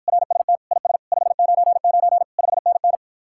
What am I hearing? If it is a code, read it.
Morse code, heard as a keyed tone, about 58 wpm, 700 Hz. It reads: RST IS 599 5NN